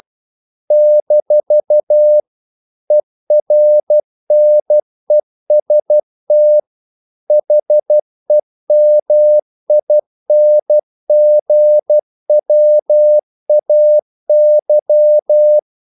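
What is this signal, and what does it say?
Morse code, heard as a keyed tone, about 12 wpm, 600 Hz. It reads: - ERNEST HEMINGWAY
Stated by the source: Morse practice sending English quotations.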